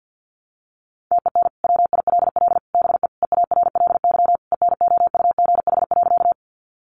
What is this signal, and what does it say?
Morse code, heard as a keyed tone, about 50 wpm, 700 Hz. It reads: TEN WILL BE EARLY ENOUGH.
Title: Sentences from Top 400 Words 50wpm